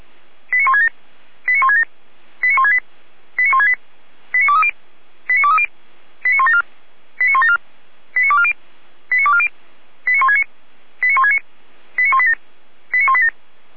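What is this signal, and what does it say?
Five-tone selective call (ZVEI), heard as quick runs of five tones, tones of about 70 ms. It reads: 89167 89167 89168 89168 89230 89230 89174 89174 89240 89240 89169 89169 89187 89187
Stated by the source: ZVEI five-tone call 89164 received by SDR on channel 455, recorded by feuerwehr-orlamuende.org fire brigade server.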